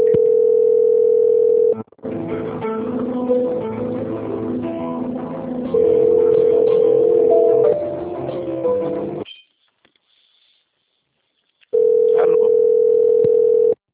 Ringback tone (North America), 440 + 480 Hz.
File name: rg-1600-unknown- - - .9204